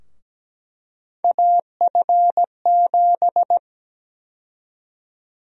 Morse code, heard as a keyed tone, about 17 wpm, 700 Hz. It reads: AF7
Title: USA Call Sign Prefixes - 17wpm